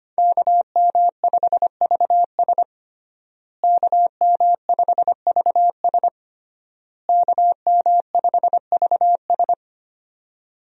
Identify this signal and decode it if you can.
Morse code, heard as a keyed tone, about 25 wpm, 700 Hz. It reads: KM5VS KM5VS KM5VS
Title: USA Call Signs - 5 Characters - Repeated 3x - 25wpm